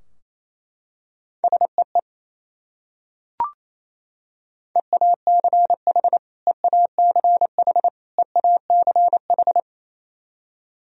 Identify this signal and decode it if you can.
Morse code, heard as a keyed tone, about 28 wpm, 700 Hz. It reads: SEE EACH EACH EACH
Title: Top 100 Words Repeated 28wpm